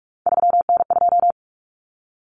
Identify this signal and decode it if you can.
Morse code, heard as a keyed tone, about 44 wpm, 700 Hz. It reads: 3N2
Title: Common Cut Numbers - 3-Digit 45wpm